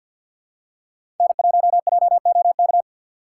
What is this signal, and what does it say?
Morse code, heard as a keyed tone, about 47 wpm, 700 Hz. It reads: N1JOK
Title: USA Call Signs - 6x Character Spacing - 50wpm